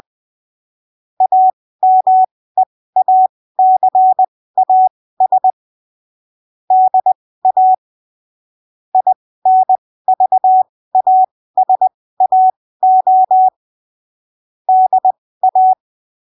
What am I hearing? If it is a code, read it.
Morse code, heard as a keyed tone, about 20 wpm, spaced as for 15 wpm, 750 Hz. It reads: AMEACAS DA INVASAO DA
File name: arquivos / texto / texto 30 20ppm